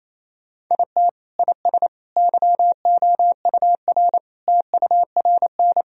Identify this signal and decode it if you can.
Morse code, heard as a keyed tone, about 28 wpm, 700 Hz. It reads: IT IS YOUR TURN
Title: Sentences from Top 1000 Words - Review Part 2 - 28wpm